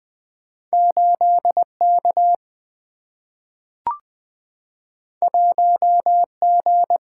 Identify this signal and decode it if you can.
Morse code, heard as a keyed tone, about 20 wpm, 700 Hz. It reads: 8K 1G
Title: Single-Letter Single-Number 20wpm